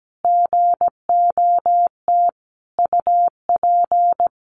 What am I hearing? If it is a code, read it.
Morse code, heard as a keyed tone, about 17 wpm, 700 Hz. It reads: GOT UP
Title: Sentences from Top 1100 Words 17wpm